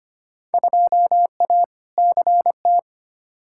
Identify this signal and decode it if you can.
Morse code, heard as a keyed tone, about 25 wpm, 700 Hz. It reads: 2A CT
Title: Field Day Exchanges - 25wpm